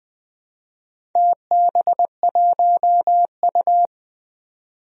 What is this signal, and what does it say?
Morse code, heard as a keyed tone, about 21 wpm, 700 Hz. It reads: TB1U
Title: DXCC-100 Call Signs - 4 Characters - Repeated 2x - 20wpm